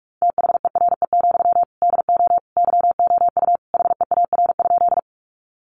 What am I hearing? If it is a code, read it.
Morse code, heard as a keyed tone, about 44 wpm, 700 Hz. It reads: THERE, DO YOU HEAR?
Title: Sentences from Top 300 Words 45wpm